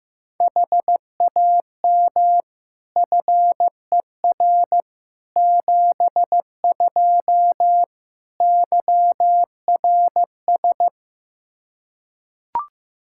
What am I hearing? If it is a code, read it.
Morse code, heard as a keyed tone, about 15 wpm, 700 Hz. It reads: HAM FER 72 YRS
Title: Practice QSO - Exchange 3 - More Info - 15wpm